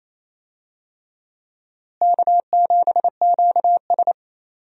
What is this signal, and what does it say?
Morse code, heard as a keyed tone, about 28 wpm, 700 Hz. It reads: K7QS
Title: USA Call Signs - 4 Characters - Repeated 3x - 28wpm